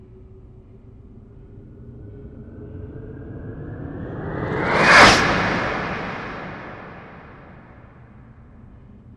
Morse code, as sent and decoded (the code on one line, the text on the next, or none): none